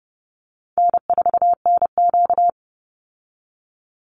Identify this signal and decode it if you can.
Morse code, heard as a keyed tone, about 30 wpm, 700 Hz. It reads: N4NQ
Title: Call Signs Repeated 3x - 30wpm